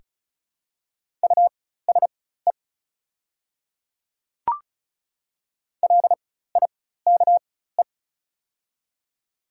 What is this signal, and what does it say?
Morse code, heard as a keyed tone, about 34 wpm, spaced as for 16 wpm, 690 Hz. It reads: USE LIKE
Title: 4x Character Spacing - Top 100 Words - 35wpm